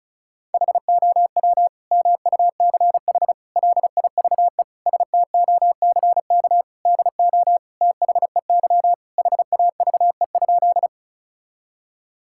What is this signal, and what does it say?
Morse code, heard as a keyed tone, about 36 wpm, 700 Hz. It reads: HOW MUCH LIVE STOCK DO THEY HAVE?
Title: Sentences from Top 1000 Words - Review Part 5 - 35wpm